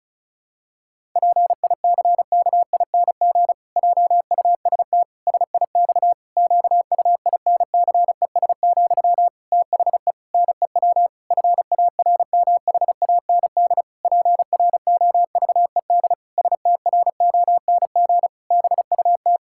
Morse code, heard as a keyed tone, about 36 wpm, 700 Hz. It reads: PICKING JUST SIX QUINCES, THE NEW FARMHAND PROVED STRONG BUT
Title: Pangram Challenge v2 - 35wpm